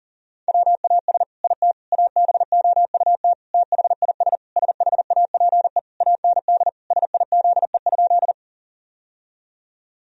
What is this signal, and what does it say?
Morse code, heard as a keyed tone, about 40 wpm, 700 Hz. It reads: WAS IT ABOUT THIS SHAPE AND SIZE?
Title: Sentences from Top 1000 Words - Review Part 3 - 40wpm